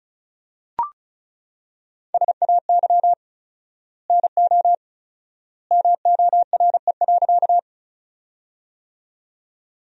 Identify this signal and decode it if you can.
Morse code, heard as a keyed tone, about 36 wpm, 700 Hz. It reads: SAY NO MORE.
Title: 4x Word-Spacing - Sentences from Top 200 Words - 35wpm